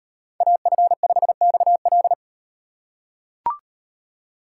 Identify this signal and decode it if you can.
Morse code, heard as a keyed tone, about 36 wpm, 700 Hz. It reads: AF5XL